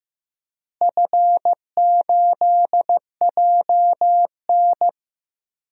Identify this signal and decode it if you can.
Morse code, heard as a keyed tone, about 15 wpm, 700 Hz. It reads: F8JN
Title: DXCC-100 Call Signs - 4 Characters - Repeated 3x - 15wpm